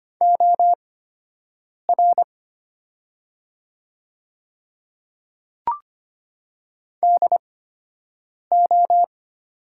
Morse code, heard as a keyed tone, about 25 wpm, spaced as for 7 wpm, 700 Hz. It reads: OR DO
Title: 8x Character Spacing - Top 100 Words - 25wpm